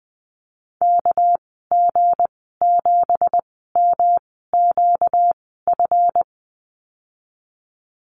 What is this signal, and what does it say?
Morse code, heard as a keyed tone, about 20 wpm, spaced as for 15 wpm, 700 Hz. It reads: KG7MQF
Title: 238 Lesson 38g - Letter q - Call Signs 20wpm